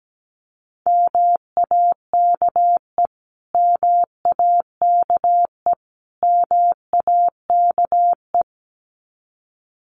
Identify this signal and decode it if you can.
Morse code, heard as a keyed tone, about 17 wpm, 700 Hz. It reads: MAKE MAKE MAKE
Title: Top 500 Words Repeated 17wpm